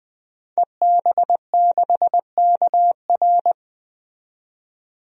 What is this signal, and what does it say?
Morse code, heard as a keyed tone, about 20 wpm, 700 Hz. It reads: EB6KR
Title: DXCC-100 Call Signs - 5 Characters - 20wpm